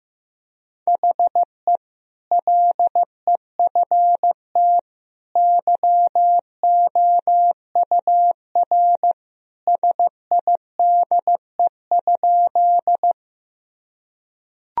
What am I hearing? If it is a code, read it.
Morse code, heard as a keyed tone, about 15 wpm, 700 Hz. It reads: HE LEFT YOUR SIDE?
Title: Sentences from Top 1000 Words - Review Part 1 - 15wpm